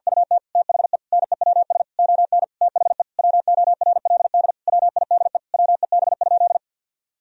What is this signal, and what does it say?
Morse code, heard as a keyed tone, about 50 wpm, 700 Hz. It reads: UT THE NEWS ON THE WORLD WIDE WEB?
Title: Sentences from Top 1000 Words - Review Part 4 - 50wpm